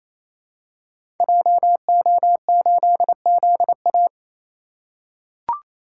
Morse code, heard as a keyed tone, about 27 wpm, 700 Hz. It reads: JO8ZA
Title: Code Groups - 5 Letter:Numbers - 28wpm